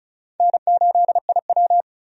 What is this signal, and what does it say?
Morse code, heard as a keyed tone, about 36 wpm, 700 Hz. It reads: N8IW
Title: USA Call Signs - 4 Characters - 35wpm